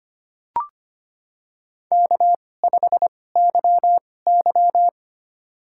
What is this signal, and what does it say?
Morse code, heard as a keyed tone, about 25 wpm, spaced as for 18 wpm, 700 Hz. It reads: K5YY
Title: USA Call Signs - 4 Characters - 2x Character Spacing - 25wpm